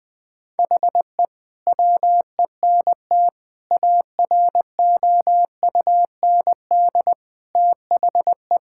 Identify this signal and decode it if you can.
Morse code, heard as a keyed tone, about 20 wpm, 700 Hz. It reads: HE WENT AROUND THE
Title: Sentences from Top 1000 Words - Review Part 5 - 20wpm